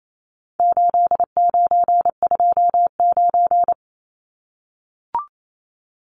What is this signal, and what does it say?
Morse code, heard as a keyed tone, about 28 wpm, 700 Hz. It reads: 8929
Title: Numbers - 4-Digit 28wpm